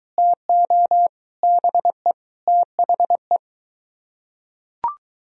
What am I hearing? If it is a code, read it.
Morse code, heard as a keyed tone, about 20 wpm, 700 Hz. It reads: TO BE THE